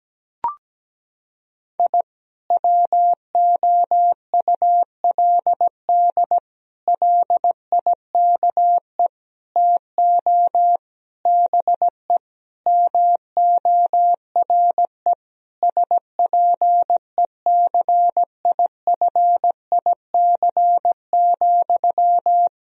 Morse code, heard as a keyed tone, about 17 wpm, 700 Hz. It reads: I WOULD LIKE TO BE MORE SPECIFIC,
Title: Sentences from Top 1000 Words - Review Part 3 - 17wpm